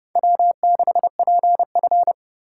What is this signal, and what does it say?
Morse code, heard as a keyed tone, about 31 wpm, 700 Hz. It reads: W6PF